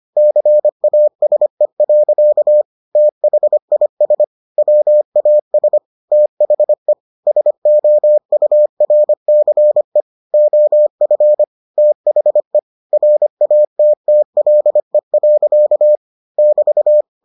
Morse code, heard as a keyed tone, about 25 wpm, 600 Hz. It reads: CASE. THIS WAS THE SOURCE OF THE RATTLE. <BT>